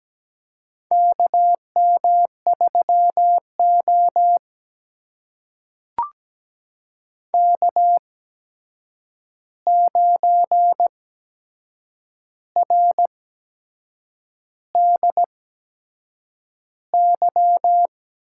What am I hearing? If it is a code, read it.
Morse code, heard as a keyed tone, about 17 wpm, spaced as for 7 wpm, 700 Hz. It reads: KM3O K9RDY
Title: USA Call Signs - 8x Character Spacing - 17wpm